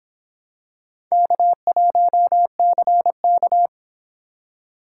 Morse code, heard as a keyed tone, about 26 wpm, 700 Hz. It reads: K1CK